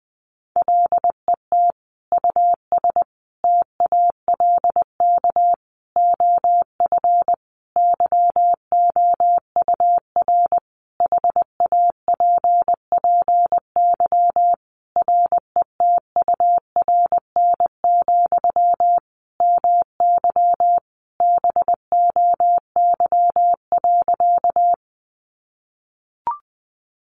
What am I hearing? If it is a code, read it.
Morse code, heard as a keyed tone, about 20 wpm, 700 Hz. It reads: LET US TALK OF YOUR HAPPY RETURN, MY BOY.
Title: Sentences from Top 1000 Words 20wpm